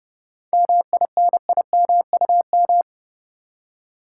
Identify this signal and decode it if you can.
Morse code, heard as a keyed tone, about 32 wpm, 700 Hz. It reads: MINIMUM
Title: Seven Letter Words - 30wpm